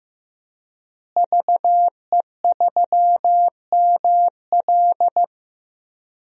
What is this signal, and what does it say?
Morse code, heard as a keyed tone, about 15 wpm, 700 Hz. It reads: VE3ML
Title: DXCC-100 Call Signs - 5 Characters - Repeated 3x - 15wpm